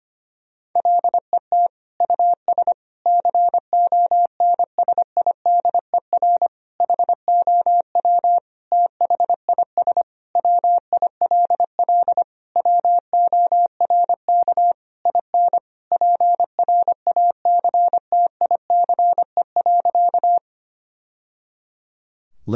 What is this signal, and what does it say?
Morse code, heard as a keyed tone, about 25 wpm, 700 Hz. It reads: LET US CONSIDER HOW THIS WILL WORK IN PRACTICE.
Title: Sentences from Top 1000 Words - Review Part 4 - 25wpm